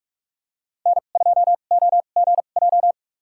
Morse code, heard as a keyed tone, about 44 wpm, spaced as for 32 wpm, 700 Hz. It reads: N2OGJ
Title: USA Call Signs - 5 Characters - 2x Character Spacing - 45wpm